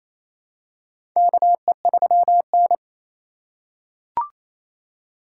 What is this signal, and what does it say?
Morse code, heard as a keyed tone, about 28 wpm, 700 Hz. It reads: KE3N
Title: USA Call Signs - 4 Characters - 2x Character Spacing - 28wpm